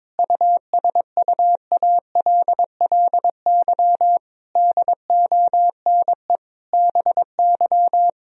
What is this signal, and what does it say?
Morse code, heard as a keyed tone, about 21 wpm, 700 Hz. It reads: USUALLY DONE BY
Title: Sentences from Top 1000 Words - Review Part 3 - 22wpm